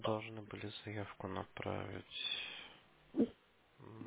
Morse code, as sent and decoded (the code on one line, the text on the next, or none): none